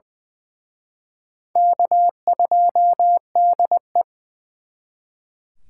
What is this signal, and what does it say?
Morse code, heard as a keyed tone, about 24 wpm, 700 Hz. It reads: K2DE
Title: Speed Racing - USA 1x2 Call Signs 20wpm